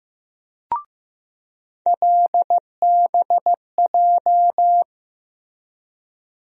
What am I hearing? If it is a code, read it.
Morse code, heard as a keyed tone, about 15 wpm, 700 Hz. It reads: LBJ
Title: USA Call Sign Postfixes - 3 Letters 15wpm